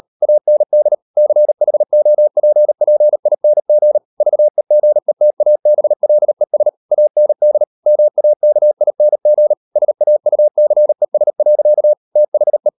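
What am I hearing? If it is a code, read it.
Morse code, heard as a keyed tone, about 38 wpm, 600 Hz. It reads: AND CHOPPING VEGETABLES AND MAKING SAUCES. THE
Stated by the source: Morse code practice at 38 WPM, sending Ken Follett's novel Never in English, from CW med Gnister.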